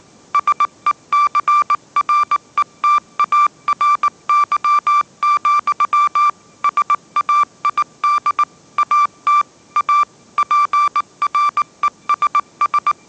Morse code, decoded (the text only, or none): SECRETARY, SAID AT A PRESS